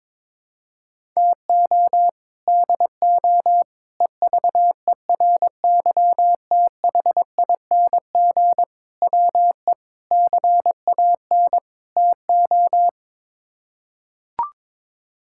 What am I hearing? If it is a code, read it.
Morse code, heard as a keyed tone, about 22 wpm, 700 Hz. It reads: TO DO EVERYTHING WE CAN TO
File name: practice / 22wpm / Sets of 6 Words - 22wpm